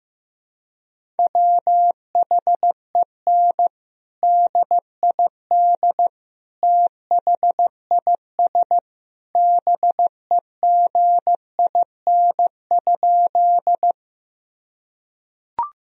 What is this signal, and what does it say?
Morse code, heard as a keyed tone, about 15 wpm, 700 Hz. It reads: WHEN DID THIS BEGIN?
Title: Sentences from Top 1000 Words - Review Part 1 - 15wpm